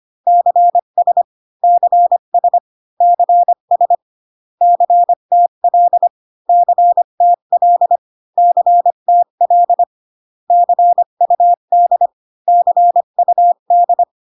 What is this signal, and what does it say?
Morse code, heard as a keyed tone, about 25 wpm, 700 Hz. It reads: CS CS CS CTL CTL CTL CUD CUD